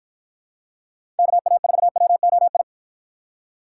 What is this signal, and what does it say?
Morse code, heard as a keyed tone, about 50 wpm, 700 Hz. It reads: KA4WOI